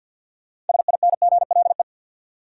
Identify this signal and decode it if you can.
Morse code, heard as a keyed tone, about 48 wpm, 700 Hz. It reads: SINGLE
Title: Sentences from Top 700 Words - New Word Drill - 50wpm